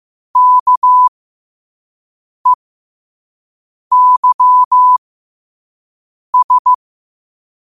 Morse code, heard as a keyed tone, about 15 wpm, spaced as for 5 wpm, 1000 Hz. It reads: KEYS
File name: qotdfiles / 5wpm